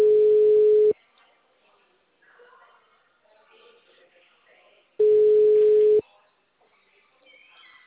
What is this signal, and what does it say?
Ringback tone (Europe, 425 Hz).
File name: out- -2005